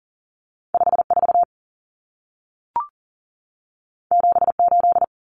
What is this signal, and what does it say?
Morse code, heard as a keyed tone, about 40 wpm, 700 Hz. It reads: 54 78
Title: Numbers - 2-Digit 40wpm